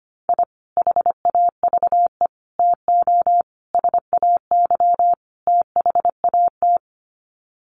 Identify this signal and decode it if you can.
Morse code, heard as a keyed tone, about 25 wpm, 700 Hz. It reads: I HAVE TO SAY THAT
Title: Sets of 5 Words Encore 25wpm